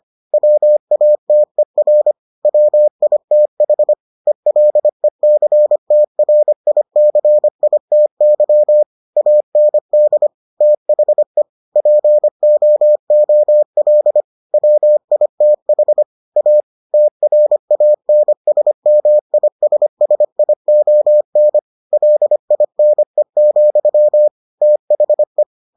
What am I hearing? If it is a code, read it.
Morse code, heard as a keyed tone, about 25 wpm, 600 Hz. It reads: WATER WITH ELECTRICITY AND THE POOL WITH A TRANSMISSION LINE, THE